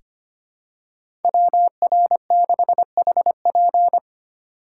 Morse code, heard as a keyed tone, about 25 wpm, 700 Hz. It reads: WR6HP
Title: USA Call Signs - 5 Characters - 3x Character Spacing - 25wpm